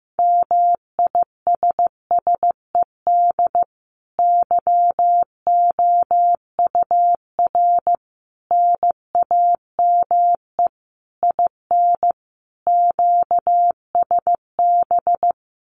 Morse code, heard as a keyed tone, about 15 wpm, 700 Hz. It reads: MISSED YOUR NAME IN QSB
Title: Common QSO Phrases 15wpm